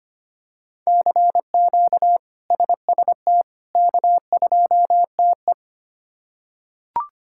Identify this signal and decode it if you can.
Morse code, heard as a keyed tone, about 25 wpm, 700 Hz. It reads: CQ SST K2TE
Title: K1USN SST QSO Practice - 25wpm